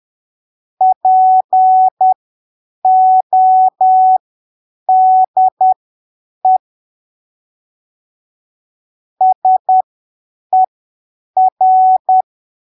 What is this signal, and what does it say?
Morse code, heard as a keyed tone, about 10 wpm, spaced as for 7 wpm, 750 Hz. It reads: PODE SER
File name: arquivos / texto / texto 18 10ppm